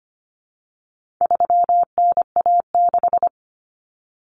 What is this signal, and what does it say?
Morse code, heard as a keyed tone, about 25 wpm, 700 Hz. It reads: 3NA6
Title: Common Cut Numbers - 4-Digit 25wpm